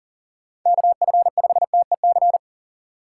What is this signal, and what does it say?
Morse code, heard as a keyed tone, about 40 wpm, 700 Hz. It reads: KF5TEC
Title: USA Call Signs - 6 Characters - 4x Character Spacing - 40wpm